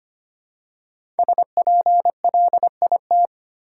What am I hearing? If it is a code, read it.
Morse code, heard as a keyed tone, about 25 wpm, 700 Hz. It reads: SPLIT
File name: 5 Letter Words - First 3 Letters Given - 25wpm